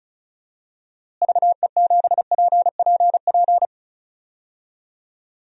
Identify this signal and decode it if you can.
Morse code, heard as a keyed tone, about 36 wpm, 700 Hz. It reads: VE7PPP
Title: DXCC-100 Call Signs - 6 Characters - Repeated 3x - 35wpm